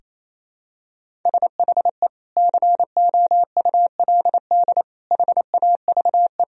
Morse code, heard as a keyed tone, about 28 wpm, 700 Hz. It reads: SHE COULD HAVE
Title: Sentences from Top 1000 Words - Review Part 1 - 28wpm